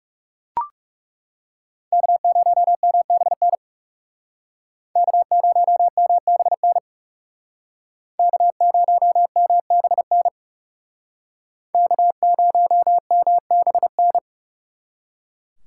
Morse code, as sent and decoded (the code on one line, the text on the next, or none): -.- ----- -- -... -. / -.- ----- -- -... -. / -.- ----- -- -... -. / -.- ----- -- -... -.
K0MBN K0MBN K0MBN K0MBN